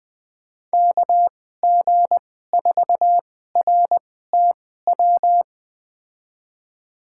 Morse code, heard as a keyed tone, about 20 wpm, spaced as for 15 wpm, 700 Hz. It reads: KG4RTW
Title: USA Call Signs - 6 Characters - 2x Character Spacing - 20wpm